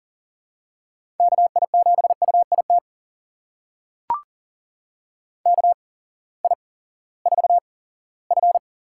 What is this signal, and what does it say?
Morse code, heard as a keyed tone, about 37 wpm, spaced as for 19 wpm, 700 Hz. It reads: KI7UIT KI4F
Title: USA Call Signs - 6 Characters - 8x Character Spacing - 40wpm